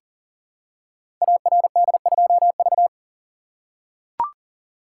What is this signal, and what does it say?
Morse code, heard as a keyed tone, about 38 wpm, 690 Hz. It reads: ARD2V
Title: Code Groups - 5 Letter:Numbers - 40wpm